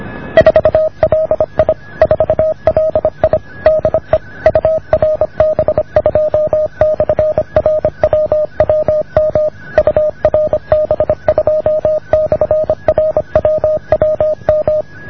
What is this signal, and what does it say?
Morse code, heard as a keyed tone, about 26 wpm, 620 Hz. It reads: 4LI 4LI DE URB2/RWWM URB2/RWWM